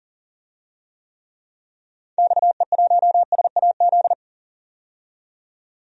Morse code, heard as a keyed tone, about 40 wpm, 700 Hz. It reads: XE1SAZ
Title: DXCC-100 Call Signs - 6 Characters - 40wpm